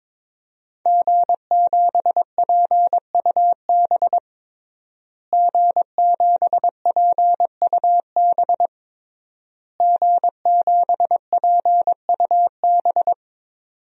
Morse code, heard as a keyed tone, about 22 wpm, 700 Hz. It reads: G7PUB G7PUB G7PUB
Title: DXCC-100 Call Signs - 5 Characters - Repeated 3x - 22wpm